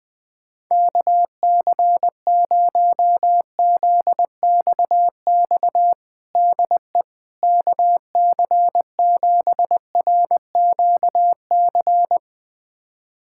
Morse code, heard as a keyed tone, about 20 wpm, 700 Hz. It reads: KC0ZXX DE KC7RQC
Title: USA Callsign DE Callsign 20wpm